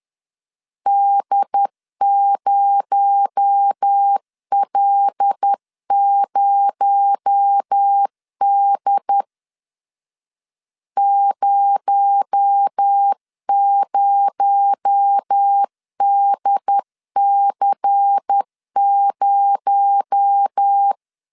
Morse code, heard as a keyed tone, about 10 wpm, 800 Hz. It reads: D0L0D 00DC0
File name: LEZIONI / 6 PKX0 / 6 PKX0 10wpm 2707